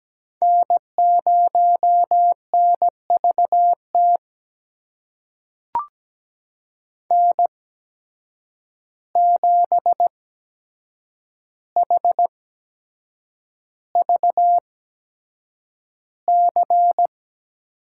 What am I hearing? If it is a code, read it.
Morse code, heard as a keyed tone, about 17 wpm, spaced as for 7 wpm, 700 Hz. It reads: N0NVT N7HVC
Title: USA Call Signs - 5 Characters - 8x Character Spacing - 17wpm